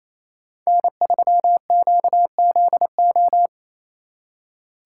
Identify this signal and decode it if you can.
Morse code, heard as a keyed tone, about 36 wpm, 700 Hz. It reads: N3QZO